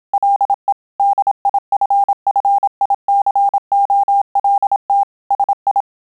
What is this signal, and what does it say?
Morse code, heard as a keyed tone, about 27 wpm, 800 Hz. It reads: LE DIFFICOLT SI